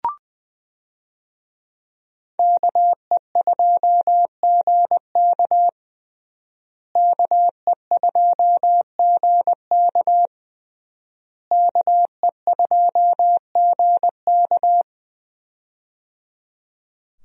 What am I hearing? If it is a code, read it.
Morse code, heard as a keyed tone, about 20 wpm, 700 Hz. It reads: KE2GK KE2GK KE2GK
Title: USA Call Signs - 5 Characters - Repeated 3x - 20wpm